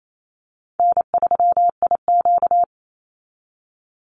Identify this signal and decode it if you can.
Morse code, heard as a keyed tone, about 28 wpm, 700 Hz. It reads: N3IQ
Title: Call Signs Repeated 3x - 28wpm